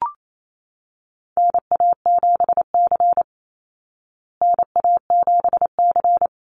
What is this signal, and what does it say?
Morse code, heard as a keyed tone, about 28 wpm, 700 Hz. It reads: NA7C NA7C
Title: Call Signs Repeated 2x - 28wpm